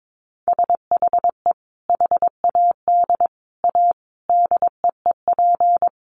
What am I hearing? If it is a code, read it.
Morse code, heard as a keyed tone, about 22 wpm, 700 Hz. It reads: SHE HAD A DEEP
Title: Sentences from Top 1300 Words 22wpm